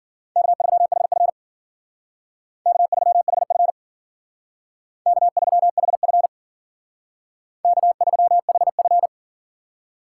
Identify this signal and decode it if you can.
Morse code, heard as a keyed tone, about 52 wpm, 700 Hz. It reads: K3HF K3HF K3HF K3HF